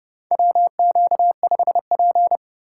Morse code, heard as a keyed tone, about 30 wpm, 700 Hz. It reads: WQ5P
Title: USA Call Signs - 4 Characters - 3x Character Spacing - 30wpm